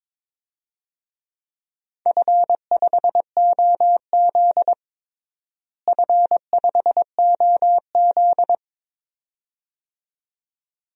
Morse code, heard as a keyed tone, about 22 wpm, 700 Hz. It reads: F5OZ F5OZ
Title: DXCC-100 Call Signs - 4 Characters - Repeated 2x - 22wpm